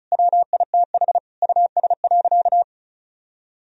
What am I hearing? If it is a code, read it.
Morse code, heard as a keyed tone, about 36 wpm, 700 Hz. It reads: WITH US.
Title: Sentences from Top 1000 Words - Review Part 4 - 35wpm